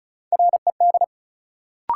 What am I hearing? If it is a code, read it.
Morse code, heard as a keyed tone, about 36 wpm, 700 Hz. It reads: RED
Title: Sentences from Top 700 Words - New Word Drill - 35wpm